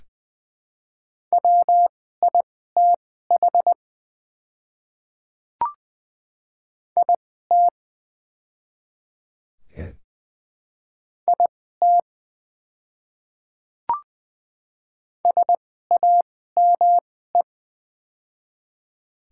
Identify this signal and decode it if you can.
Morse code, heard as a keyed tone, about 20 wpm, spaced as for 14 wpm, 700 Hz. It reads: WITH IT IT SAME
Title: 2x Character Spacing - Top 200 Words - 20wpm